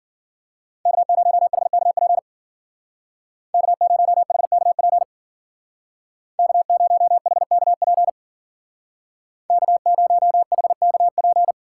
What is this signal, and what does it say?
Morse code, heard as a keyed tone, about 53 wpm, 690 Hz. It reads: K0HKP K0HKP K0HKP K0HKP